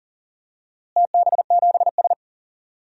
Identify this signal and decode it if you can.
Morse code, heard as a keyed tone, about 40 wpm, 700 Hz. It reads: TB7S
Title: DXCC-100 Call Signs - 4 Characters - Repeated 3x - 40wpm